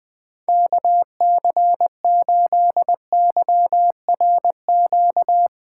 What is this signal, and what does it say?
Morse code, heard as a keyed tone, about 22 wpm, 700 Hz. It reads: KC8YRQ
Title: Speed Racing - USA 2x3 Call Signs 20wpm